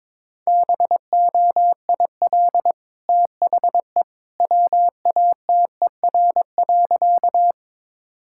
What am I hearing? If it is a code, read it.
Morse code, heard as a keyed tone, about 22 wpm, 700 Hz. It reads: BOIL THE WATER.
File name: practice / 22wpm / Easy Sentences with Hints - 22wpm